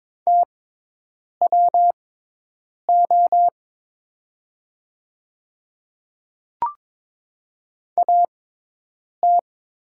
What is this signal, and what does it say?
Morse code, heard as a keyed tone, about 22 wpm, spaced as for 8 wpm, 700 Hz. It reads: TWO AT
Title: 6x Character Spacing - Top 200 Words - 22wpm